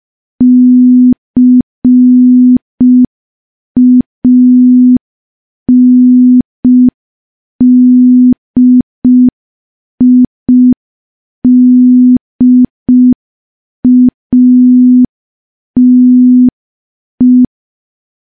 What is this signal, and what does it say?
Morse code, heard as a keyed tone, about 5 wpm, 250 Hz. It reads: CANDIDATE
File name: morse / morse-training-05WPM